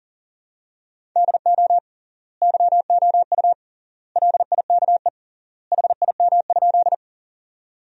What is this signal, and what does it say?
Morse code, heard as a keyed tone, about 37 wpm, 700 Hz. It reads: DO YOU LIKE HIM?
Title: 3x Word-Spacing - Sentences from Top 100 Words - 40wpm